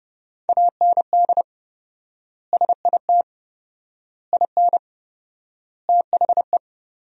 Morse code, heard as a keyed tone, about 30 wpm, 700 Hz. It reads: AND SIT IN THE
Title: 4x Word-Spacing - Sentences from Top 500 Words - 30wpm